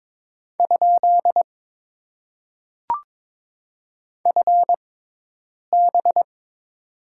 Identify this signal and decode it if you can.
Morse code, heard as a keyed tone, about 22 wpm, spaced as for 8 wpm, 700 Hz. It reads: ? FB
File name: Top 90 QSO Elements - 6x Character Spacing - 22wpm